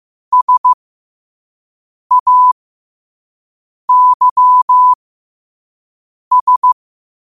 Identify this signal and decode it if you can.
Morse code, heard as a keyed tone, about 15 wpm, spaced as for 5 wpm, 1000 Hz. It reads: SAYS